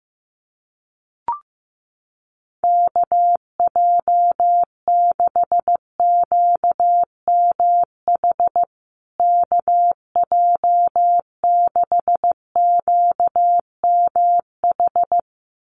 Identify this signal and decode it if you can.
Morse code, heard as a keyed tone, about 15 wpm, 700 Hz. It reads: KJ6QMH KJ6QMH
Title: USA Call Signs Repeated 2x 15wpm